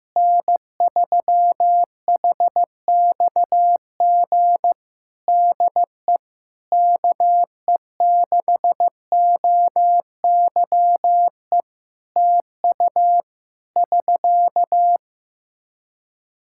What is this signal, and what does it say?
Morse code, heard as a keyed tone, about 15 wpm, 700 Hz. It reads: N3HXG DE KE6OYE TU <SK>
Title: Practice QSO - Exchange 4 - Ending - 15wpm